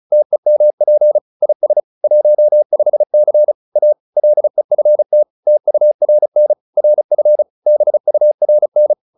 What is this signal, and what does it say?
Morse code, heard as a keyed tone, about 35 wpm, 600 Hz. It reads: TEMP IS 15C A LEFT TURN RF BURN